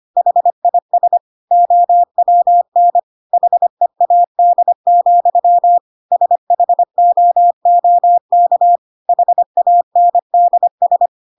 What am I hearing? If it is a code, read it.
Morse code, heard as a keyed tone, about 25 wpm, 700 Hz. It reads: HIS OWN HEAD, SHOOK HANDS
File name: Pinocchio / Pinocchio ch2 25 25 ik1hge